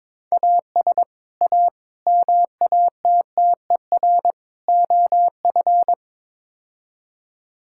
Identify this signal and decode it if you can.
Morse code, heard as a keyed tone, about 22 wpm, 700 Hz. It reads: AS A MATTER OF
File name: practice / 22wpm / Build A Phrase - 5-Word Phrases - 22wpm